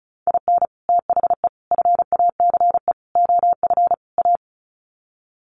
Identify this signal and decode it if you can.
Morse code, heard as a keyed tone, about 34 wpm, 700 Hz. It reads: IN THE FACE OF A
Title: Sets of 5 Words Encore 35wpm